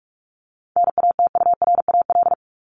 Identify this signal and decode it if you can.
Morse code, heard as a keyed tone, about 44 wpm, 700 Hz. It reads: NATURAL
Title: Top 501-750 Words 45wpm